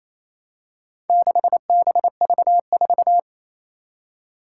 Morse code, heard as a keyed tone, about 28 wpm, 700 Hz. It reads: 6BV4
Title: Four Challenging Letter-Number Combinations 28wpm